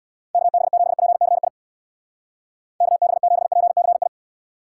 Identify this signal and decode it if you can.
Morse code, heard as a keyed tone, about 64 wpm, 700 Hz. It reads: KB7PCI KB7PCI